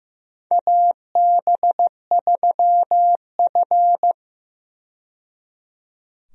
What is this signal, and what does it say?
Morse code, heard as a keyed tone, about 15 wpm, 700 Hz. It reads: AB3F
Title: USA Call Signs - 4 Characters - 15wpm